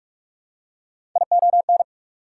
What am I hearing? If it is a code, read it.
Morse code, heard as a keyed tone, about 43 wpm, 690 Hz. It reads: ION
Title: Prefixes and Suffixes 45wpm